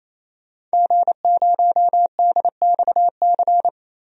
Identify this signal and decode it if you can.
Morse code, heard as a keyed tone, about 28 wpm, 700 Hz. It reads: G0DXC
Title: DXCC-100 Call Signs - 5 Characters - 28wpm